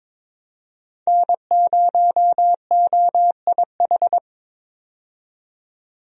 Morse code, heard as a keyed tone, about 22 wpm, 700 Hz. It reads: N0OIH
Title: USA Call Signs - 5 Characters - 22wpm